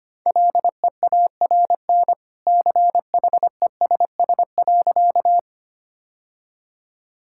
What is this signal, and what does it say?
Morse code, heard as a keyed tone, about 25 wpm, 700 Hz. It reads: LEARN CHESS.
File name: Easy Sentences with Hints - 25wpm